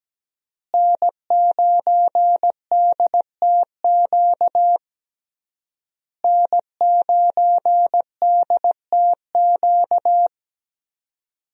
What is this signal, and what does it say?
Morse code, heard as a keyed tone, about 17 wpm, 700 Hz. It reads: N9DTQ N9DTQ
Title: USA Call Signs - 5 Characters - Repeated 3x - 17wpm